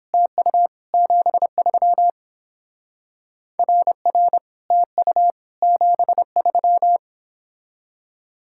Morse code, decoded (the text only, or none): TU 73 RR TU 73